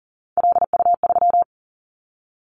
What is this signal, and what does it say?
Morse code, heard as a keyed tone, about 40 wpm, 690 Hz. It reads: LU3
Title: Callsign Prefixes for Top 100 DXCC - 40wpm